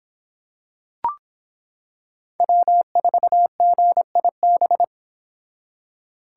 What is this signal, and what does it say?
Morse code, heard as a keyed tone, about 26 wpm, 700 Hz. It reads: W4GIB